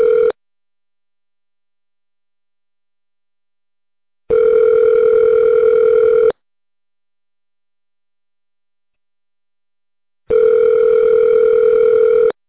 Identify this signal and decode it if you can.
Ringback tone (North America), 440 + 480 Hz.